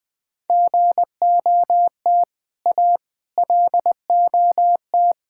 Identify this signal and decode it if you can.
Morse code, heard as a keyed tone, about 20 wpm, 700 Hz. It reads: GOT A LOT
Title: Build A Phrase - 4-Word Phrases - 20wpm